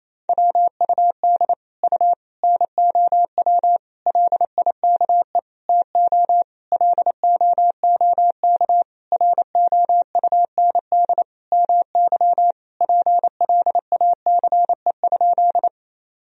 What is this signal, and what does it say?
Morse code, heard as a keyed tone, about 27 wpm, 700 Hz. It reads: WUD U NOW LIKE TO LOOK ROUND MY PLACE?
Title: Sentences from Top 200 Words and Abbreviations 28wpm